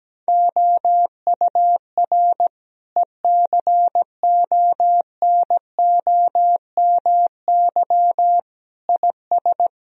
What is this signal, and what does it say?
Morse code, heard as a keyed tone, about 17 wpm, 700 Hz. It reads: OUR ECONOMY IS
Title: Sentences from Top 1000 Words - Review Part 3 - 17wpm